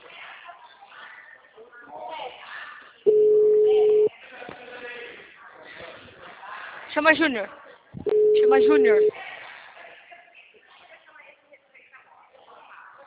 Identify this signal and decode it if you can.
Ringback tone (Europe, 425 Hz), with speech.